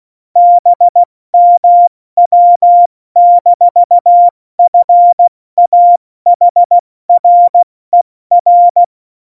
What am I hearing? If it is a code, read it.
Morse code, heard as a keyed tone, about 16 wpm, spaced as for 14 wpm, 700 Hz. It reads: BMW-FAHRER